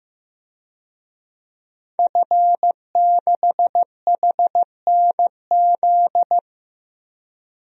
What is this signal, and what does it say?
Morse code, heard as a keyed tone, about 15 wpm, 700 Hz. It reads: F6HNZ
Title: DXCC-100 Call Signs - 5 Characters - 15wpm